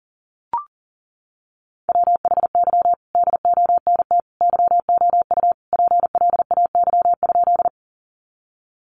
Morse code, heard as a keyed tone, about 40 wpm, 700 Hz. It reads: WHY DONT YOU PLAY?
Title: Sentences from Top 300 Words 40wpm